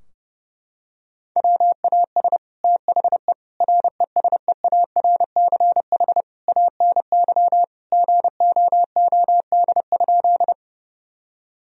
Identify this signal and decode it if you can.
Morse code, heard as a keyed tone, about 32 wpm, 700 Hz. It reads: WAS THE RESEARCH ANY GOOD?
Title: Sentences from Top 1000 Words - Review Part 4 - 30wpm